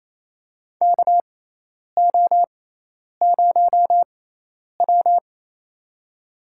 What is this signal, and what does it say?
Morse code, heard as a keyed tone, about 28 wpm, spaced as for 10 wpm, 700 Hz. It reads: KO0W
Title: USA Call Signs - 6x Character Spacing - 28wpm